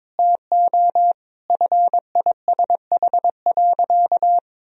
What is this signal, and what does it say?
Morse code, heard as a keyed tone, about 22 wpm, 700 Hz. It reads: TO FISH.
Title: Sentences from Top 1000 Words - Review Part 1 - 22wpm